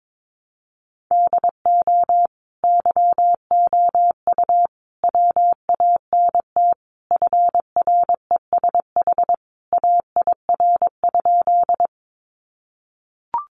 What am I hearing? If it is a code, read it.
Morse code, heard as a keyed tone, about 22 wpm, 700 Hz. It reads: DO YOU WANT FRESH AIR?
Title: Sentences from Top 1200 Words 22wpm